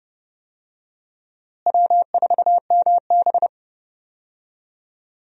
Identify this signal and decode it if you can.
Morse code, heard as a keyed tone, about 31 wpm, 700 Hz. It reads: W4MB
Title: USA Call Signs - 4 Characters - Repeated 2x - 30wpm